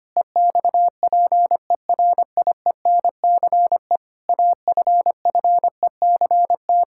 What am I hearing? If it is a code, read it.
Morse code, heard as a keyed tone, about 25 wpm, 700 Hz. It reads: EXPERIENCE AFFECT
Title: Sentences from Top 1000 Words - Review Part 6 - 25wpm